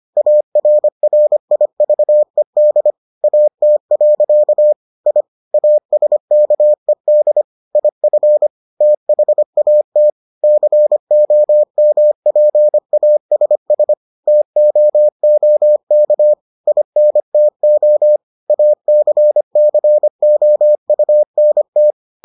Morse code, heard as a keyed tone, about 25 wpm, 600 Hz. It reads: ARRIVED AT. I ASKED IF THAT COMPASS TOOK INTO ACCOUNT